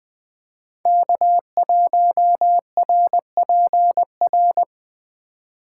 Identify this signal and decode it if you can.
Morse code, heard as a keyed tone, about 19 wpm, 700 Hz. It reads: K1RPR